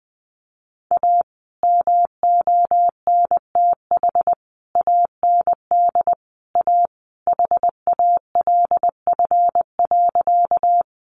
Morse code, heard as a keyed tone, about 19 wpm, 700 Hz. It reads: A MONTH AND A HALF.
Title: Sentences from Top 800 Words 20wpm